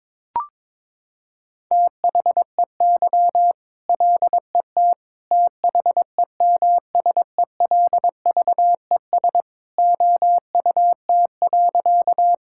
Morse code, heard as a keyed tone, about 22 wpm, 700 Hz. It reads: THEY LET THEMSELVES OUT.
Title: Sentences from Top 1000 Words - Review Part 5 - 22wpm